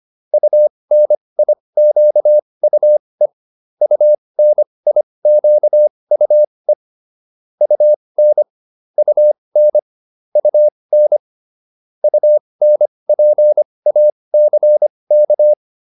Morse code, heard as a keyed tone, about 25 wpm, spaced as for 20 wpm, 600 Hz. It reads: UNIQUE UNIQUE UN UN UN UNPACK